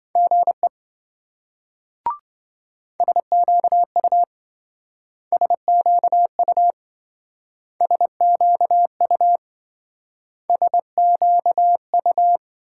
Morse code, heard as a keyed tone, about 25 wpm, 700 Hz. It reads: GE SQU SQU SQU SQU
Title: Speed-Racing Multi-letter Phonemes - 20wpm